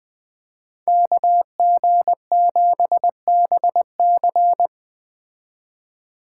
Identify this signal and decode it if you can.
Morse code, heard as a keyed tone, about 21 wpm, 700 Hz. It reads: KG7BC